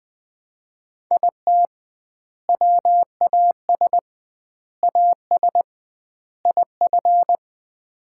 Morse code, heard as a keyed tone, about 20 wpm, 700 Hz. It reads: IT WAS AS IF
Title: 2x Word-Spacing - Sets of 4 Words - 20wpm